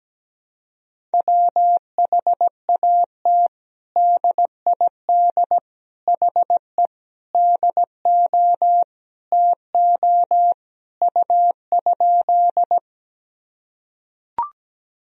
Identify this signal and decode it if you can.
Morse code, heard as a keyed tone, about 17 wpm, 700 Hz. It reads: WHAT DID HE DO TO U?
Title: Sentences from Top 100 Words and Abbreviations 17wpm